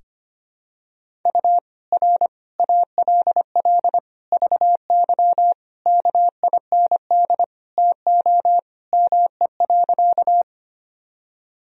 Morse code, heard as a keyed tone, about 25 wpm, 700 Hz. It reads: U R ALL VY KIND TO ME.
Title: Sentences from Top 200 Words and Abbreviations 25wpm